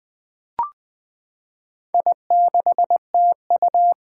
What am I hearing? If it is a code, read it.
Morse code, heard as a keyed tone, about 21 wpm, 700 Hz. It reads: I6TU
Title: Four Challenging Letter-Number Combinations 20wpm